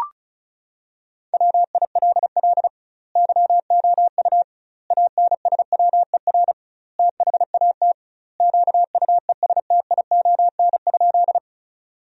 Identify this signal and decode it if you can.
Morse code, heard as a keyed tone, about 34 wpm, 700 Hz. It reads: WILL YOU ANSWER THAT QUESTION?
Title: 2x Word-Spacing - Sentences from Top 400 Words - 35wpm